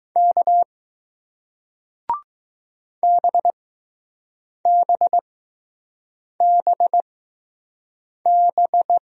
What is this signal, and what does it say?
Morse code, heard as a keyed tone, about 20 wpm, spaced as for 6 wpm, 700 Hz. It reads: K BBBB